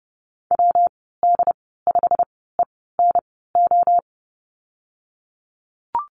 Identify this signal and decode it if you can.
Morse code, heard as a keyed tone, about 30 wpm, spaced as for 17 wpm, 700 Hz. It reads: WD5ENO
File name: Lesson 19g - Letter w - Call Signs 30wpm